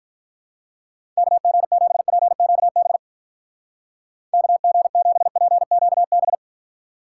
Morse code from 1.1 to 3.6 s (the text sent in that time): KG7PQB